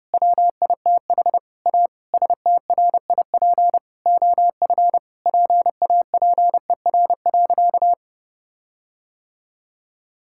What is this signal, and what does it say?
Morse code, heard as a keyed tone, about 30 wpm, 700 Hz. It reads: WITH A STRIP OF PAPER.
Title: Sentences from Top 1600 Words 30wpm